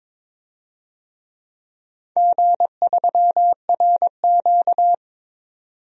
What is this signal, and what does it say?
Morse code, heard as a keyed tone, about 22 wpm, 700 Hz. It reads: G3RQ
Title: DXCC-100 Call Signs - 4 Characters - 22wpm